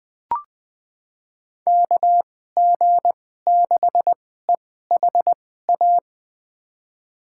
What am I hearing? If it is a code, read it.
Morse code, heard as a keyed tone, about 20 wpm, spaced as for 14 wpm, 700 Hz. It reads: KG6EHA
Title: USA Call Signs - 6 Characters - 2x Character Spacing - 20wpm